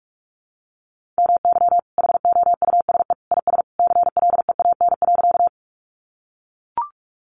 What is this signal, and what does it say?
Morse code, heard as a keyed tone, about 44 wpm, 700 Hz. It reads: MY HOUSE IS CLEAN.
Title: Sentences from Top 1100 Words 45wpm